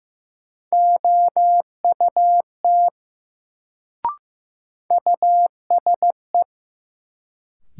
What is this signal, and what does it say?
Morse code, heard as a keyed tone, about 15 wpm, 700 Hz. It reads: OUT USE
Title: Sentences from Top 100 Words - New Word Drill - 15wpm